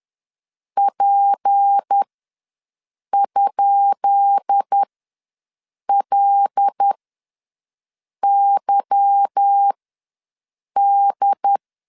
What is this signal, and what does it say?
Morse code, heard as a keyed tone, about 10 wpm, spaced as for 6 wpm, 800 Hz. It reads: P?LYD
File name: LEZIONI / 10 SIGNS / 10B INT ws 10wpm 4134